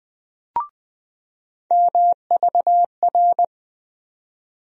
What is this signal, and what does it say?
Morse code, heard as a keyed tone, about 20 wpm, 700 Hz. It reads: MVR
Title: USA Call Sign Postfixes - 3 Letters 20wpm